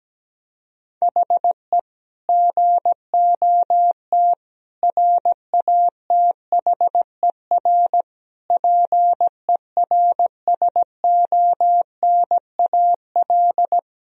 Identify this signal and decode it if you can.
Morse code, heard as a keyed tone, about 17 wpm, 700 Hz. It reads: HE GOT RATHER PERSONAL
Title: Sentences from Top 1000 Words - Review Part 4 - 17wpm